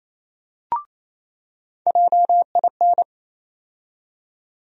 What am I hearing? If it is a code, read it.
Morse code, heard as a keyed tone, about 28 wpm, 700 Hz. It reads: JIN
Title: USA Call Sign Postfixes - 3 Letters 28wpm